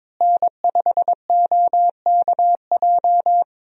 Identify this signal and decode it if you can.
Morse code, heard as a keyed tone, about 22 wpm, 700 Hz. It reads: N5OKJ
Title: USA Call Signs - 5 Characters - Repeated 2x - 22wpm